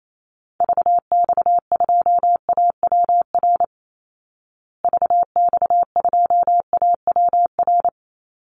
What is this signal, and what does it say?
Morse code, heard as a keyed tone, about 28 wpm, 700 Hz. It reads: VX2AWR VX2AWR
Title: Call Signs Repeated 2x - 28wpm